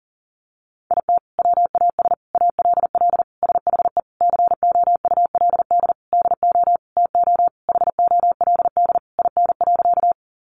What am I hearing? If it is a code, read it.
Morse code, heard as a keyed tone, about 40 wpm, 700 Hz. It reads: IT WAS ALL SHE COULD DO TO HOLD IN.
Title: Sentences from Top 400 Words 40wpm